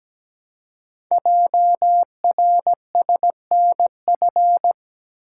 Morse code, heard as a keyed tone, about 17 wpm, 700 Hz. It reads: JRSNF
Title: Code Groups - 5 Letters - 17wpm